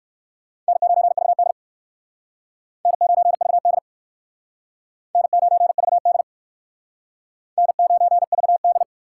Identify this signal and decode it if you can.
Morse code, heard as a keyed tone, about 56 wpm, 700 Hz. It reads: N9VD N9VD N9VD N9VD
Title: Speed Racing - USA Call Signs 45wpm